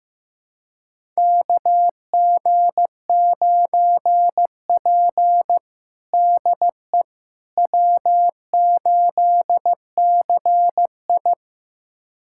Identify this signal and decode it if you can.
Morse code, heard as a keyed tone, about 15 wpm, 700 Hz. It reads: KG9P DE W8CI
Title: USA Callsign DE Callsign - 4 Characters - 15wpm